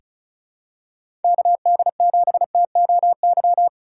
Morse code, heard as a keyed tone, about 34 wpm, 700 Hz. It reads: KD7TOY
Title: USA Call Signs - 4x Character Spacing - 35wpm